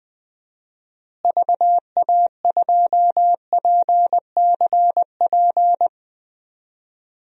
Morse code, heard as a keyed tone, about 20 wpm, 700 Hz. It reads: VA2PCP
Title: DXCC-100 Call Signs - 6 Characters - Repeated 3x - 20wpm